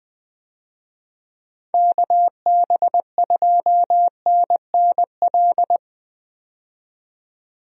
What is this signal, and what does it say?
Morse code, heard as a keyed tone, about 20 wpm, 700 Hz. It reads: KB2NNL
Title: USA Call Signs - 6 Characters - 20wpm